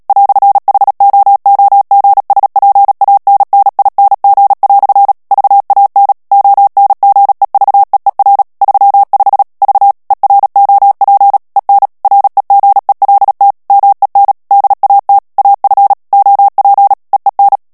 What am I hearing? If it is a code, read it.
Morse code, heard as a keyed tone, about 37 wpm, 800 Hz. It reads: "HOOGSPANNING" VAN ONGEVEER 35 V EROP EN REGELT MEN DAT AF OP EEN
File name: morse mp3 / HOLLANDAIS texte clair 2 vitesse 35 mots minute